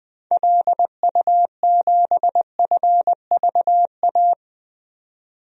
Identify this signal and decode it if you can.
Morse code, heard as a keyed tone, about 20 wpm, 700 Hz. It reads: LU7FVA
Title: DXCC-100 Call Signs - 6 Characters - 20wpm